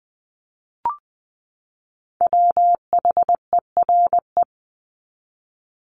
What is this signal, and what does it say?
Morse code, heard as a keyed tone, about 20 wpm, 700 Hz. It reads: WHERE
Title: Top 300 Words 20wpm